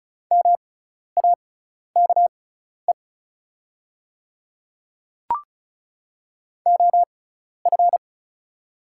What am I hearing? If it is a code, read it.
Morse code, heard as a keyed tone, about 36 wpm, spaced as for 12 wpm, 700 Hz. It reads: MAKE OF
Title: 6x Character Spacing - Top 100 Words - 35wpm